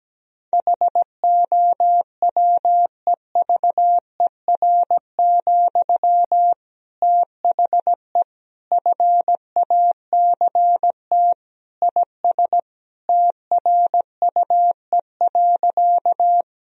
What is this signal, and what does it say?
Morse code, heard as a keyed tone, about 17 wpm, 700 Hz. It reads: HOWEVER, THE FACT IS TRUE.
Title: Sentences from Top 1000 Words - Review Part 3 - 17wpm